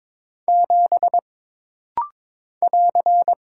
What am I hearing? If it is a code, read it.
Morse code, heard as a keyed tone, about 22 wpm, 700 Hz. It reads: 7 <AR>
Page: This rapid-fire timing is geared to push your ICR (Instant Character Recognition) skills.